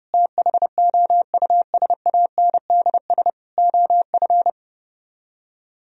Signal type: Morse code

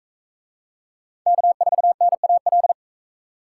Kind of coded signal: Morse code

